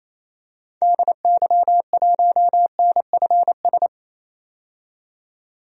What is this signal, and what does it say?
Morse code, heard as a keyed tone, about 28 wpm, 700 Hz. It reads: DY1NFS
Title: DXCC-100 Call Signs - 6 Characters - 28wpm